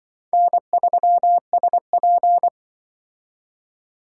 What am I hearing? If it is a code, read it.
Morse code, heard as a keyed tone, about 25 wpm, 700 Hz. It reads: N3SP
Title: Speed Racing - USA Call Signs 20wpm